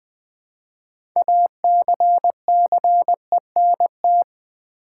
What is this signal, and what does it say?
Morse code, heard as a keyed tone, about 20 wpm, 700 Hz. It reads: ACCENT
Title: Build A Word - 3 Letters At A Time - 6-Letter Words - 20wpm